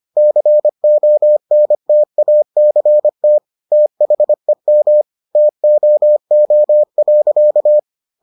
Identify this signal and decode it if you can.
Morse code, heard as a keyed tone, about 25 wpm, 600 Hz. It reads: CONTACT THEM TOO.